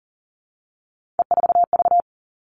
Morse code, heard as a keyed tone, about 40 wpm, 700 Hz. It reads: E4V
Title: All Cut Numbers - 3-Digit 40wpm